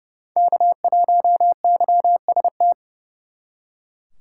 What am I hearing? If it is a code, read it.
Morse code, heard as a keyed tone, about 37 wpm, 700 Hz. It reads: K1YST
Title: Speed Racing - USA Call Signs 30wpm